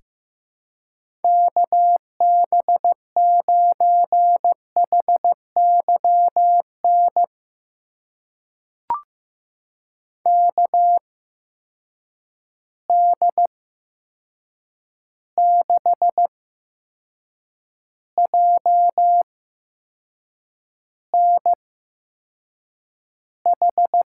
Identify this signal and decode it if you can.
Morse code, heard as a keyed tone, about 15 wpm, spaced as for 6 wpm, 700 Hz. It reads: KB9HYN KD6JNH
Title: USA Call Signs - 6 Characters - 8x Character Spacing - 15wpm